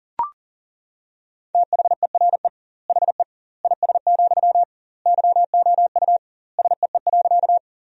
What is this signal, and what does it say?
Morse code, heard as a keyed tone, about 40 wpm, 700 Hz. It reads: THERE HE IS, YOU SEE.
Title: 2x Word-Spacing - Sentences from Top 100 Words - 40wpm